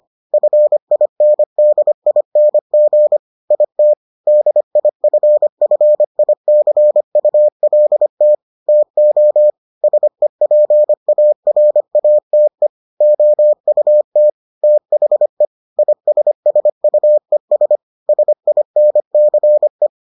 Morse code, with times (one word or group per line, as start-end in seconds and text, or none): 0.3-3.2 FINDING
3.5-3.9 IT
4.3-8.3 DIFFICULT
8.7-9.5 TO
9.8-12.7 SEPARATE
13.0-14.3 OUT
14.6-15.4 THE
15.8-17.8 ISSUES
18.1-19.9 SINCE